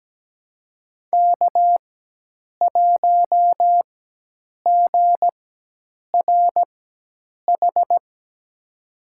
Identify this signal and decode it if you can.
Morse code, heard as a keyed tone, about 17 wpm, spaced as for 8 wpm, 700 Hz. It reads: K1GRH
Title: USA Call Signs - 5 Characters - 4x Character Spacing - 17wpm